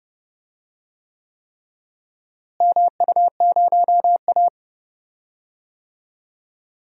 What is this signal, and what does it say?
Morse code, heard as a keyed tone, about 30 wpm, 700 Hz. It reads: MU0A
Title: DXCC-100 Call Signs - 4 Characters - Repeated 3x - 30wpm